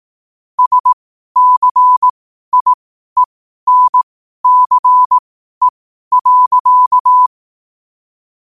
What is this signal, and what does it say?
Morse code, heard as a keyed tone, about 18 wpm, spaced as for 12 wpm, 1000 Hz. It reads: SCIENCE.
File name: qotdfiles / 12wpm